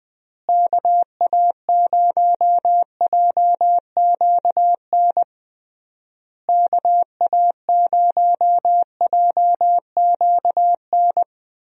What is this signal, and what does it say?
Morse code, heard as a keyed tone, about 20 wpm, 700 Hz. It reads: KA0JQN KA0JQN
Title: USA Call Signs - 6 Characters - Repeated 3x - 20wpm